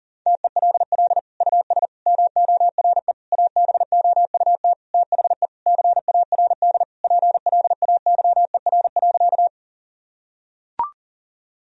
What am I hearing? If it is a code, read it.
Morse code, heard as a keyed tone, about 40 wpm, 700 Hz. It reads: TELL US MORE ABOUT THE CARD PLAYER.
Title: Sentences from Top 1000 Words - Review Part 4 - 40wpm